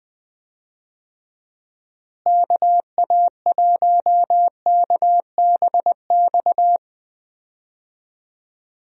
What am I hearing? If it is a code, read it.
Morse code, heard as a keyed tone, about 20 wpm, 700 Hz. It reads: KA1KBX
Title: USA Call Signs - 6 Characters - Repeated 2x - 20wpm